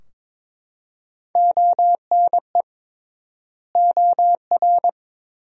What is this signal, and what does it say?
Morse code, heard as a keyed tone, about 22 wpm, 700 Hz. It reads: ONE OR
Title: 3x Word-Spacing - Sentences from Top 100 Words - 22wpm